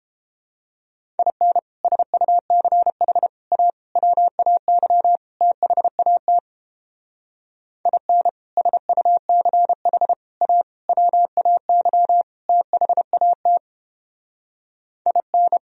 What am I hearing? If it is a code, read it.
Morse code, heard as a keyed tone, about 29 wpm, 700 Hz. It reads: IN SUCH A WAY THAT IN SUCH A WAY THAT IN